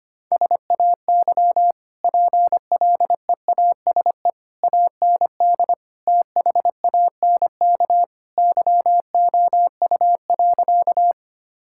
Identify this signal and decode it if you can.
Morse code, heard as a keyed tone, about 25 wpm, 700 Hz. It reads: SAY PLEASE AND THANK YOU.
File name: Easy Sentences with Hints - 25wpm